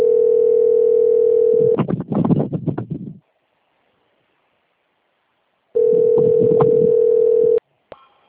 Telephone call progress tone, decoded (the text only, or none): ringback tone